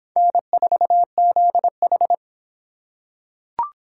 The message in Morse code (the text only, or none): N4ZH